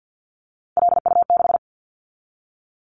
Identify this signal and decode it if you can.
Morse code, heard as a keyed tone, about 50 wpm, 700 Hz. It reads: LU6
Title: Callsign Prefixes for Top 100 DXCC - 50wpm